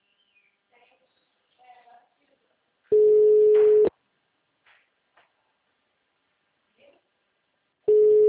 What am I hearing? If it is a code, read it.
Ringback tone (Europe, 425 Hz), with speech.
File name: out- -2002- - - .61534